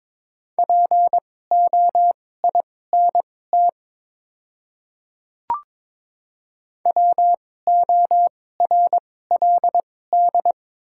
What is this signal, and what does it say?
Morse code, heard as a keyed tone, about 22 wpm, spaced as for 16 wpm, 700 Hz. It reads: POINT WORLD
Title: 2x Character Spacing - Top 200 Words - 22wpm